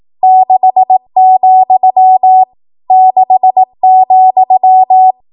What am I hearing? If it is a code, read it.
Morse code, heard as a keyed tone, about 18 wpm, 750 Hz. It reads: 6, 6,
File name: Morse / 18WPM